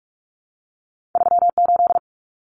Morse code, heard as a keyed tone, about 46 wpm, 700 Hz. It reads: 38
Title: Common Cut Numbers - 2-Digit 45wpm